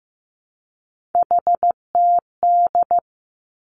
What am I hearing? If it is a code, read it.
Morse code, heard as a keyed tone, about 15 wpm, 700 Hz. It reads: HTD
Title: Three Challenging Letter-Number Combinations 15wpm